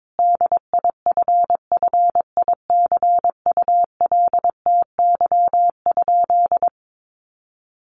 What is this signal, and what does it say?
Morse code, heard as a keyed tone, about 22 wpm, 700 Hz. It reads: DIFFICULTY?
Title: Sentences from Top 1500 Words 22wpm